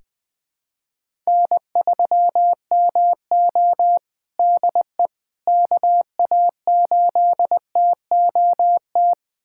Morse code, heard as a keyed tone, about 20 wpm, 700 Hz. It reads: N3MO DE KA8TOT
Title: USA Callsign DE Callsign 20wpm